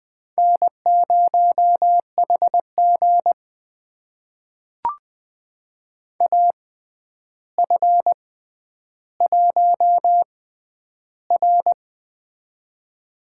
Morse code, heard as a keyed tone, about 20 wpm, spaced as for 10 wpm, 700 Hz. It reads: N0HG AF1R
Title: USA Call Signs - 4 Characters - 6x Character Spacing - 20wpm